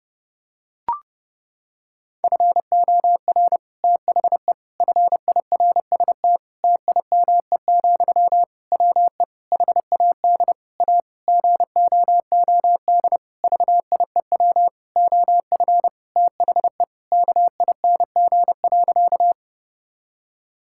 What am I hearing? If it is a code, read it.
Morse code, heard as a keyed tone, about 30 wpm, 700 Hz. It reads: FOR THE FIRST TIME, WE HAD A GOOD VIEW OF THE KING.
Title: Sentences from Top 1000 Words - Review Part 6 - 30wpm